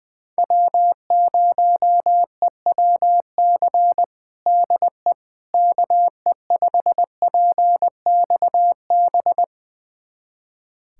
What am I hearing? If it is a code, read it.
Morse code, heard as a keyed tone, about 21 wpm, 690 Hz. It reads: W0EWC DE KE5PXB
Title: USA Callsign DE Callsign 20wpm